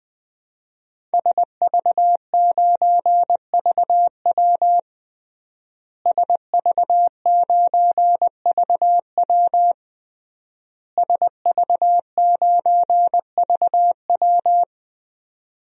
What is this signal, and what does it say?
Morse code, heard as a keyed tone, about 20 wpm, 700 Hz. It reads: SV9VW SV9VW SV9VW
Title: DXCC-100 Call Signs - 5 Characters - Repeated 3x - 20wpm